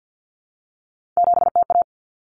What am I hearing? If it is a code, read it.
Morse code, heard as a keyed tone, about 50 wpm, 700 Hz. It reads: 7TA
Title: Common Cut Numbers - 3-Digit 50wpm